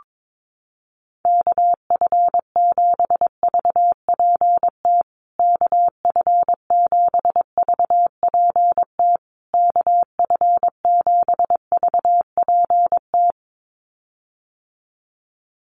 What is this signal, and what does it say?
Morse code, heard as a keyed tone, about 22 wpm, 700 Hz. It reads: KF7VPT KF7VPT KF7VPT
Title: USA Call Signs Repeated 3x 22wpm